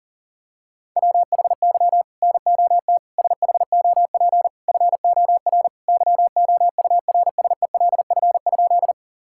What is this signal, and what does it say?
Morse code, heard as a keyed tone, about 40 wpm, 700 Hz. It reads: WHY NOT SHOP FOR YOURSELF?
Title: Sentences from Top 1600 Words 40wpm